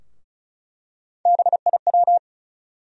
Morse code, heard as a keyed tone, about 34 wpm, 700 Hz. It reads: BIW
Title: USA Call Sign Postfixes - 3 Letters 35wpm